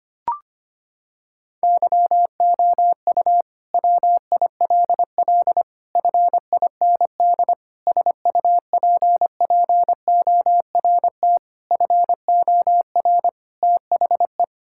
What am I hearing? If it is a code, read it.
Morse code, heard as a keyed tone, about 25 wpm, 700 Hz. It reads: YOU WILL FIND SUPPORT FOR THE
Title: Sentences from Top 1000 Words - Review Part 5 - 25wpm